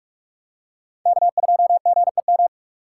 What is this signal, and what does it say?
Morse code, heard as a keyed tone, about 40 wpm, 700 Hz. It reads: K2GEM